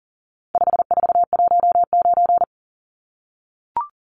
Morse code, heard as a keyed tone, about 40 wpm, 700 Hz. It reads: 5419
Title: Numbers - 4-Digit 40wpm